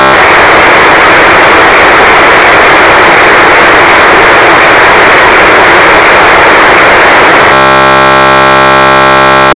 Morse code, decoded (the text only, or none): none